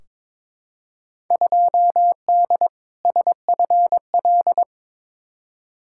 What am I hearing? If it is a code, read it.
Morse code, heard as a keyed tone, about 22 wpm, 700 Hz. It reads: 2D SFL
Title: Field Day Exchanges Encore - 22wpm